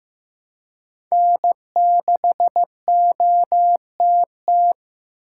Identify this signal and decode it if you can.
Morse code, heard as a keyed tone, about 15 wpm, 700 Hz. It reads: N6OTT
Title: USA Call Signs - 5 Characters - 4x Character Spacing - 15wpm